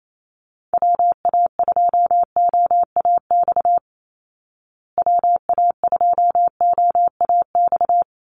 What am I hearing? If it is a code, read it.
Morse code, heard as a keyed tone, about 28 wpm, 700 Hz. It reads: WA2OAX WA2OAX
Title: Call Signs Repeated 3x - 28wpm